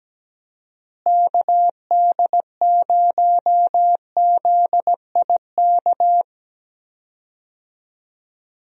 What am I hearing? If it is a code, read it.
Morse code, heard as a keyed tone, about 17 wpm, 700 Hz. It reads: KD0ZIK
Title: USA Call Signs - 6 Characters - Repeated 3x - 17wpm